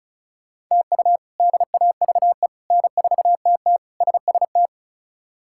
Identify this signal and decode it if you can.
Morse code, heard as a keyed tone, about 36 wpm, 700 Hz. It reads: TU DAVE N4TT SST
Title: K1USN SST QSO Practice - 35wpm